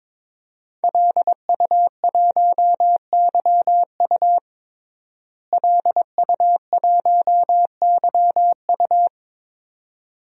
Morse code, heard as a keyed tone, about 22 wpm, 700 Hz. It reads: LU1YU LU1YU
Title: DXCC-100 Call Signs - 5 Characters - Repeated 2x - 22wpm